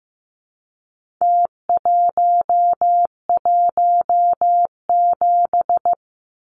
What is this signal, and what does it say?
Morse code, heard as a keyed tone, about 15 wpm, 700 Hz. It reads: T117
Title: Numbers - Contest Style - 4-Digit 15wpm